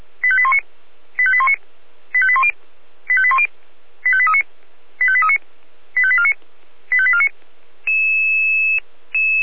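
Five-tone selective call (ZVEI), heard as quick runs of five tones, tones of about 70 ms. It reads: 85819 85819 85810 85810 85839 85839 85849 85849
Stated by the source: ZVEI five-tone call received with an SDR.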